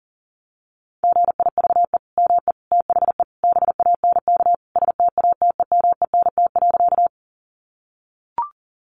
Morse code, heard as a keyed tone, about 38 wpm, 700 Hz. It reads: GIVE ME THE BANK STATEMENT.
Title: Sentences from Top 1200 Words 40wpm